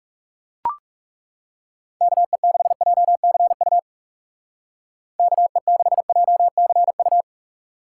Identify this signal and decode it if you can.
Morse code, heard as a keyed tone, about 40 wpm, 700 Hz. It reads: KE6JCU KE6JCU